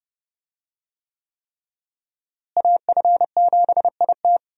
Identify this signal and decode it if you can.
Morse code, heard as a keyed tone, about 30 wpm, 700 Hz. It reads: AF7IT
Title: USA Call Signs - 5 Characters - Repeated 3x - 30wpm